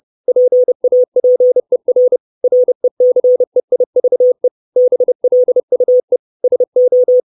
Morse code, heard as a keyed tone, about 30 wpm, 500 Hz. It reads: PAPER RECEIVE BLUE SO
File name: 632 Words 30WPM